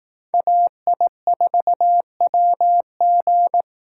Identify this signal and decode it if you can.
Morse code, heard as a keyed tone, about 18 wpm, 700 Hz. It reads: AI4WG
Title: Speed Racing - USA Call Signs 15wpm